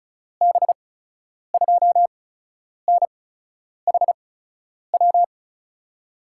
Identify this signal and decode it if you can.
Morse code, heard as a keyed tone, about 36 wpm, spaced as for 10 wpm, 700 Hz. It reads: B2NHW
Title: USA Call Signs - 8x Character Spacing - 35wpm